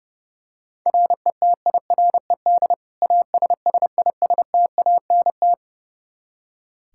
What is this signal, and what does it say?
Morse code, heard as a keyed tone, about 30 wpm, 700 Hz. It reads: RETIRED ASSISTANT
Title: Practice QSO - Exchange 3 - More Info - 30wpm